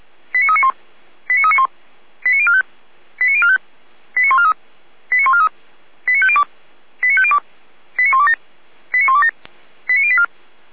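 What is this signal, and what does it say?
Five-tone selective call (ZVEI), heard as quick runs of five tones, tones of about 80 ms. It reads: 89391 89391 89045 89045 89143 89143 89502 89502 89127 89127 89084
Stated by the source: Five-tone ZVEI selective call received on SDR.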